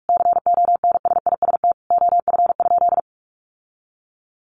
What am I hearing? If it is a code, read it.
Morse code, heard as a keyed tone, about 44 wpm, 700 Hz. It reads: CONSIST OF?
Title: Sentences from Top 1400 Words 45wpm